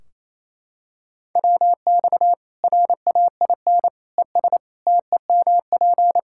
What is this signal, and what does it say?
Morse code, heard as a keyed tone, about 28 wpm, 700 Hz. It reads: WX RAIN ES TEMP
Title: Practice QSO - Exchange 2 - Info - 28wpm